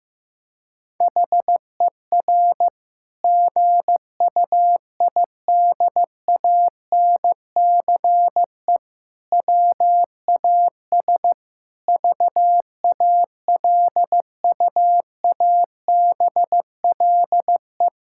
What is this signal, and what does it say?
Morse code, heard as a keyed tone, about 16 wpm, 700 Hz. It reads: HER GUIDANCE WAS VALUABLE
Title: Sentences from Top 1600 Words 15wpm